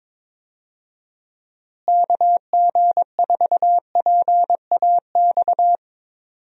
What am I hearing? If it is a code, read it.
Morse code, heard as a keyed tone, about 22 wpm, 700 Hz. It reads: KG4PAX
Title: USA Call Signs - 6 Characters - Repeated 2x - 22wpm